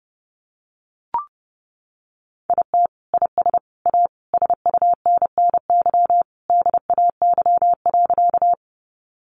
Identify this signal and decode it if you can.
Morse code, heard as a keyed tone, about 30 wpm, 700 Hz. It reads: IT IS A SUNNY DAY.
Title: Additional Easy Phrases 30wpm